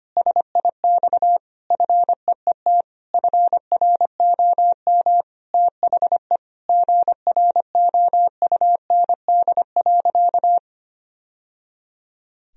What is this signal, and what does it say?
Morse code, heard as a keyed tone, about 25 wpm, 700 Hz. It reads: SIX FEET FROM THE GROUND.
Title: Sentences from Top 1000 Words - Review Part 2 - 25wpm